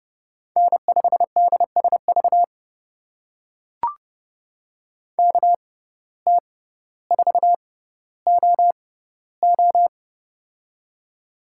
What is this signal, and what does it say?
Morse code, heard as a keyed tone, about 30 wpm, spaced as for 15 wpm, 700 Hz. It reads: N5DSV KT4OO
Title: USA Call Signs - 5 Characters - 6x Character Spacing - 30wpm